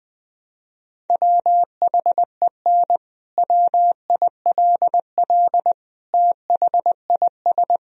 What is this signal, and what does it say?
Morse code, heard as a keyed tone, about 20 wpm, 700 Hz. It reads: WHEN WILL THIS
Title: Sentences from Top 1000 Words - Review Part 2 - 20wpm